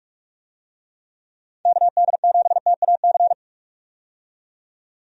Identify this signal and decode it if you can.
Morse code, heard as a keyed tone, about 46 wpm, 700 Hz. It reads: KD7TAC
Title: USA Call Signs - 6 Characters - 45wpm